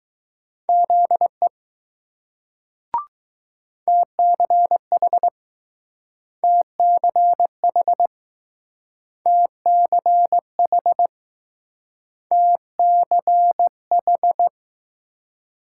Morse code, heard as a keyed tone, about 21 wpm, 700 Hz. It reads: ZE TCH TCH TCH TCH